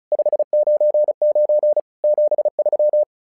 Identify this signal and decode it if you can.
Morse code, heard as a keyed tone, about 36 wpm, 600 Hz. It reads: 599 73
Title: SOTA QSO Practice 35wpm